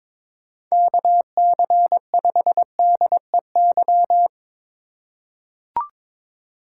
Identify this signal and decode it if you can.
Morse code, heard as a keyed tone, about 22 wpm, 700 Hz. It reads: KC5DEY
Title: USA Call Signs - 6 Characters - 4x Character Spacing - 22wpm